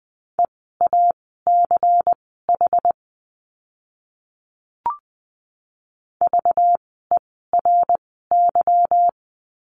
Morse code, heard as a keyed tone, about 20 wpm, spaced as for 15 wpm, 700 Hz. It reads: EACH VERY
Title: 192 Lesson 31f - Letter v - Words 20wpm